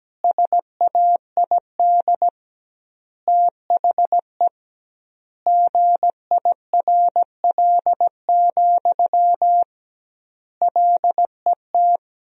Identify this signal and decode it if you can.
Morse code, heard as a keyed tone, about 17 wpm, 700 Hz. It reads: SAID THE GIRL, LET
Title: 2x Word-Spacing - Sentences from Top 400 Words - 17wpm